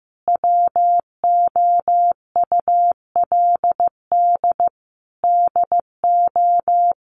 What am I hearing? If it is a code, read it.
Morse code, heard as a keyed tone, about 15 wpm, 700 Hz. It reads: WOULD DO
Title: Sentences from Top 1400 Words 15wpm